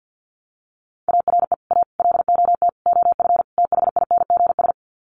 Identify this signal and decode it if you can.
Morse code, heard as a keyed tone, about 50 wpm, 700 Hz. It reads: ARE A LOT OF THINGS
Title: Sets of 5 Words Encore2 50wpm